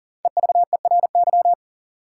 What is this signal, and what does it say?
Morse code, heard as a keyed tone, about 40 wpm, 700 Hz. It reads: EVERY
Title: Sets of 5 Words - New Word Drill - 40wpm